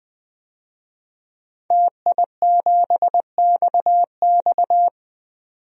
Morse code, heard as a keyed tone, about 20 wpm, 700 Hz. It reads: TI7XX
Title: DXCC-100 Call Signs - 5 Characters - 20wpm